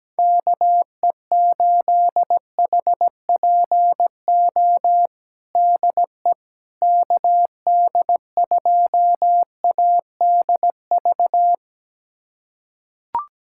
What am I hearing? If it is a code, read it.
Morse code, heard as a keyed tone, about 17 wpm, 700 Hz. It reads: KE8HPO DE KD2ADV
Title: USA Callsign DE Callsign - 6 Characters - 17wpm